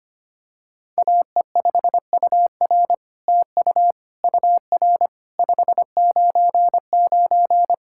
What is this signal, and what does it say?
Morse code, heard as a keyed tone, about 25 wpm, 700 Hz. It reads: AE5UR TU UR 599
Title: SOTA QSO Practice 25wpm